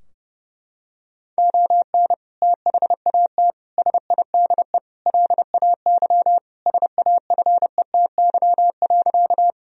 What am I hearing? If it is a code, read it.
Morse code, heard as a keyed tone, about 30 wpm, 700 Hz. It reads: ON THAT SIDE LAY SAFETY.
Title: Sentences from Top 1000 Words - Review Part 4 - 30wpm